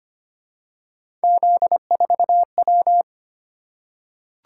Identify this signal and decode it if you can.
Morse code, heard as a keyed tone, about 25 wpm, 700 Hz. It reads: Z4W
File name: Three Letter-Number Combinations - 25wpm